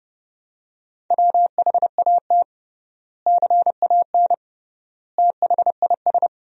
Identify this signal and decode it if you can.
Morse code, heard as a keyed tone, about 30 wpm, 700 Hz. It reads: WHAT CAN THIS
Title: 3x Word-Spacing - Sentences from Top 100 Words - 30wpm